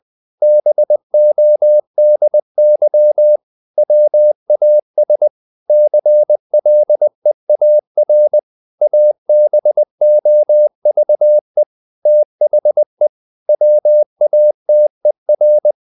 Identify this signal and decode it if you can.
Morse code, heard as a keyed tone, about 20 wpm, 600 Hz. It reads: BODY WAS CLEAR ABOVE THE WATER
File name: the old man and the sea part 2 20wpm